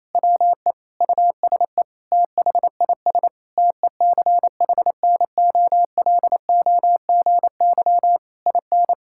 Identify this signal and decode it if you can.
Morse code, heard as a keyed tone, about 27 wpm, 700 Hz. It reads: WE USE THIS TECHNOLOGY IN
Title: Sentences from Top 1000 Words - Review Part 5 - 28wpm